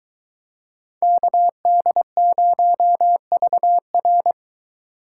Morse code, heard as a keyed tone, about 22 wpm, 700 Hz. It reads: KD0VR